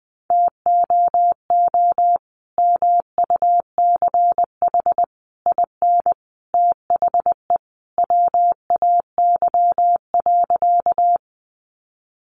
Morse code, heard as a keyed tone, about 20 wpm, 700 Hz. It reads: TOO MUCH IN THE WAY.
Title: Sentences from Top 200 Words 20wpm